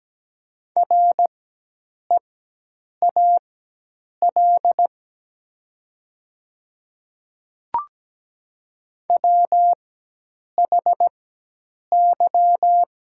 Morse code, heard as a keyed tone, about 17 wpm, spaced as for 8 wpm, 700 Hz. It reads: REAL WHY
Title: 4x Character Spacing - Top 300 Words - 17wpm